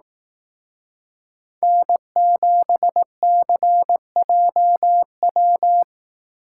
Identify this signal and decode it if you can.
Morse code, heard as a keyed tone, about 18 wpm, 700 Hz. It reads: N7CJW
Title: Speed Racing - USA 1x3 Call Signs 15wpm